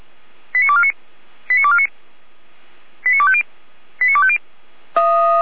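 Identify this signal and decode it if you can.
Five-tone selective call (ZVEI), heard as quick runs of five tones, tones of about 70 ms. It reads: 89249 89249 89250 89250
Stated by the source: ZVEI five-tone call 89501, received by SDR.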